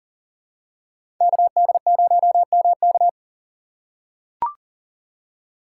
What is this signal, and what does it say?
Morse code, heard as a keyed tone, about 40 wpm, 700 Hz. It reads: KD0MK
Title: USA Call Signs - 5 Characters - 2x Character Spacing - 40wpm